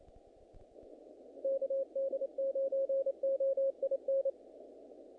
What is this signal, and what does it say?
Morse code, heard as a keyed tone, about 26 wpm, 500 Hz, with radio noise. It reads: KD9OIN
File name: contests / naqp-cw-k5tr-op-k5pi- -2000